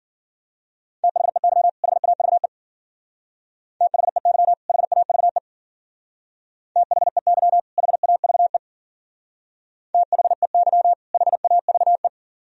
Morse code, heard as a keyed tone, about 50 wpm, 700 Hz. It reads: THEY HAVE THEY HAVE THEY HAVE THEY HAVE
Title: Speed-Racing - Sets of 2 Words 40wpm